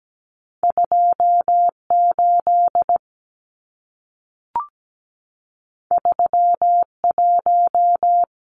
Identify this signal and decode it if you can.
Morse code, heard as a keyed tone, about 17 wpm, 700 Hz. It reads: 28 31
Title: Numbers - 2-Digit 17wpm